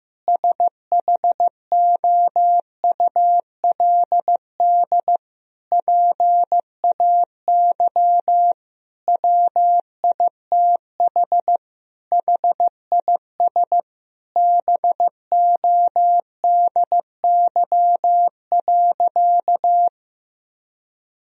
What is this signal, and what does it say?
Morse code, heard as a keyed tone, about 15 wpm, 700 Hz. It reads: SHOULD PAY WITH HIS BODY.
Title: Sentences from Top 1000 Words - Review Part 2 - 15wpm